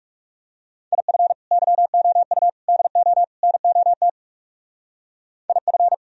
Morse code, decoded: IF YOU DO NOT IF

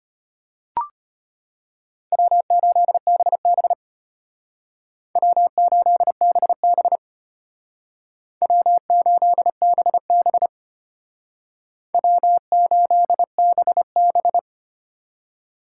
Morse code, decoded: W8BB W8BB W8BB W8BB

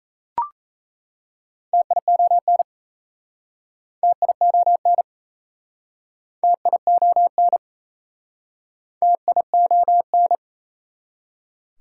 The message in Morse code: TION TION TION TION

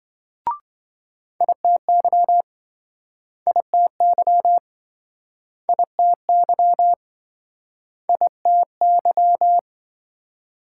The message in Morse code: ITY ITY ITY ITY

